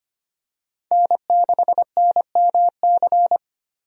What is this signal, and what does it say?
Morse code, heard as a keyed tone, about 25 wpm, 700 Hz. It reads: N6NMC
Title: USA Call Signs - 5 Characters - Repeated 2x - 25wpm